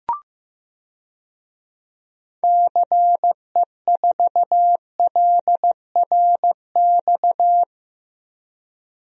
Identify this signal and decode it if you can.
Morse code, heard as a keyed tone, about 15 wpm, 700 Hz. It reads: CE4LRX
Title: DXCC-100 Call Signs - 6 Characters - 15wpm